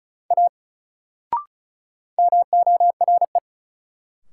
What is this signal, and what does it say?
Morse code, heard as a keyed tone, about 36 wpm, 700 Hz. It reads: A MORE
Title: Sentences from Top 100 Words - New Word Drill - 35wpm